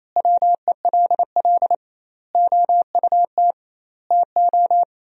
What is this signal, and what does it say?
Morse code, heard as a keyed tone, about 28 wpm, 700 Hz. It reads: WELL OUT TO
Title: 2x Word-Spacing - Sentences from Top 400 Words - 28wpm